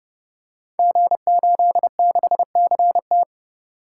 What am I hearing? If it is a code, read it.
Morse code, heard as a keyed tone, about 30 wpm, 700 Hz. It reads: G86CT
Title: Code Groups - 5 Letter:Numbers - 30wpm